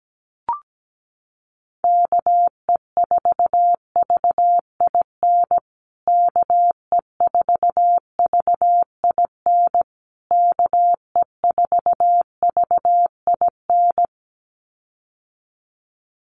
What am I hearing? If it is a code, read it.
Morse code, heard as a keyed tone, about 17 wpm, 700 Hz. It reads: KE4VIN KE4VIN KE4VIN
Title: USA Call Signs Repeated 3x 17wpm